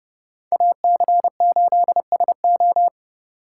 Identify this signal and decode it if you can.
Morse code, heard as a keyed tone, about 28 wpm, 700 Hz. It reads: AC8SO